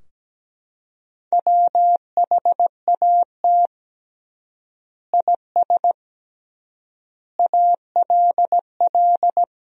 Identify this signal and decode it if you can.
Morse code, heard as a keyed tone, about 17 wpm, 700 Hz. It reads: WHAT IS ALL
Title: 3x Word-Spacing - Sentences from Top 100 Words - 17wpm